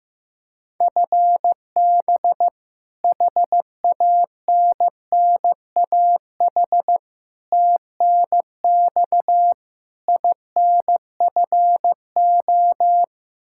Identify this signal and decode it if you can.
Morse code, heard as a keyed tone, about 16 wpm, 700 Hz. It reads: FB HANNAH TNX INFO
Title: Practice QSO - Exchange 3 - More Info - 15wpm